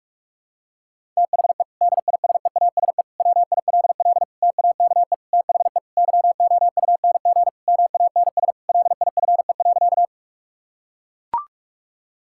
Morse code, heard as a keyed tone, about 41 wpm, 700 Hz. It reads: THE DISEASE WILL TAKE THE YOUNG MANS LIFE.
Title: Sentences from Top 1000 Words - Review Part 6 - 45wpm